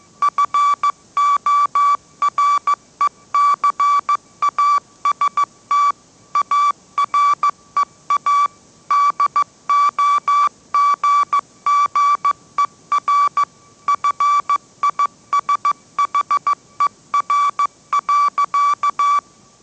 Morse code, heard as a keyed tone, about 14 wpm, 1150 Hz. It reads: FORECAST AREA DOGGER FISHER.